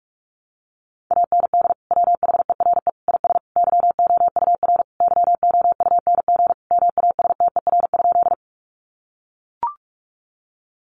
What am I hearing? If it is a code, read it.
Morse code, heard as a keyed tone, about 44 wpm, 700 Hz. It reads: AND WHERE IS YOUR YOUNG MASTER?
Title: Sentences from Top 1300 Words 45wpm